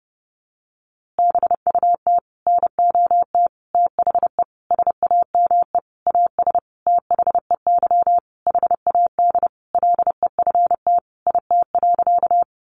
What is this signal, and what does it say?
Morse code, heard as a keyed tone, about 30 wpm, 700 Hz. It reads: BUT NOT THE SAME AS THEY HAD LEFT IT.
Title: A Sound Of Thunder 30wpm